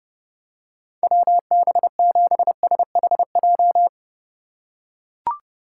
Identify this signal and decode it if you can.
Morse code, heard as a keyed tone, about 30 wpm, 700 Hz. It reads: WB7SHJ
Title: USA Call Signs - 2x Character Spacing - 30wpm